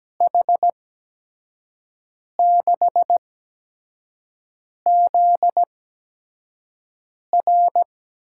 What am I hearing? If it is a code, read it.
Morse code, heard as a keyed tone, about 17 wpm, spaced as for 5 wpm, 700 Hz. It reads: H6ZR
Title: USA Call Signs - 8x Character Spacing - 17wpm